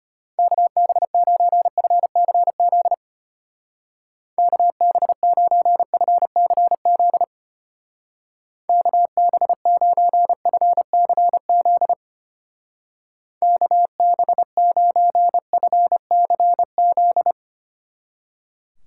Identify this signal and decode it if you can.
Morse code, heard as a keyed tone, about 31 wpm, 700 Hz. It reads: KB9FCZ KB9FCZ KB9FCZ KB9FCZ